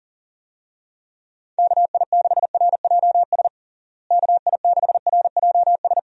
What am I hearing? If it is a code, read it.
Morse code, heard as a keyed tone, about 40 wpm, 700 Hz. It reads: KI6RJS KI6RJS
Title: USA Call Signs - 6 Characters - Repeated 2x - 40wpm